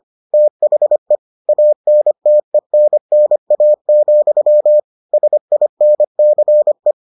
Morse code, heard as a keyed tone, about 25 wpm, 600 Hz. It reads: THE ANTENNA, SINCE